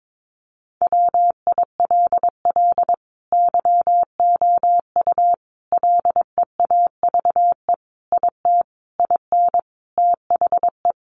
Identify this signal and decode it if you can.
Morse code, heard as a keyed tone, about 22 wpm, 700 Hz. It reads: WILL YOU LEAVE IT IN THE
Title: Sentences from Top 1300 Words 22wpm